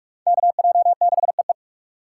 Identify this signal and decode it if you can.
Morse code, heard as a keyed tone, about 44 wpm, 690 Hz. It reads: KJ6EE
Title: USA Call Signs - 4x Character Spacing - 45wpm